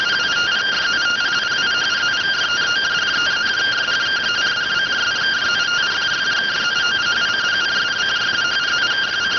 Two-tone teleprinter signal, about 170 Hz shift.